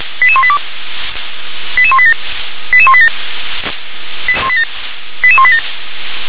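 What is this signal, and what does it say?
Five-tone selective call (ZVEI), heard as quick runs of five tones, tones of about 70 ms. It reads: 88182 88187 88187 88187
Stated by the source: ZVEI five-tone selective call received by SDR.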